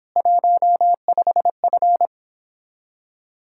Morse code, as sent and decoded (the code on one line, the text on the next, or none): .---- ..... ..-.
15F